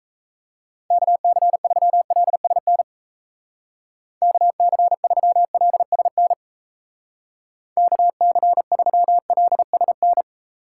Morse code, heard as a keyed tone, about 34 wpm, 700 Hz. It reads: KC3LSN KC3LSN KC3LSN
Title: Speed Racing - USA 2x3 Call Signs 28wpm